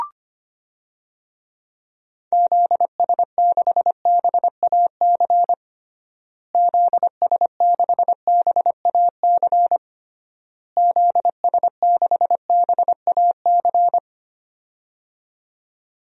Morse code, heard as a keyed tone, about 25 wpm, 700 Hz. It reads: ZS6BAC ZS6BAC ZS6BAC
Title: DXCC-100 Call Signs - 6 Characters - Repeated 3x - 25wpm